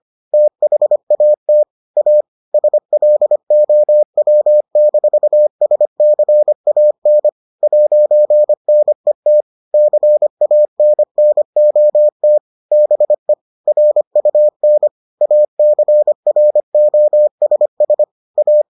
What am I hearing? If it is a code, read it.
Morse code, heard as a keyed tone, about 25 wpm, 600 Hz. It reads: THAT A SLOW-SCAN 'NET CANNOT BE RUN ACROSS A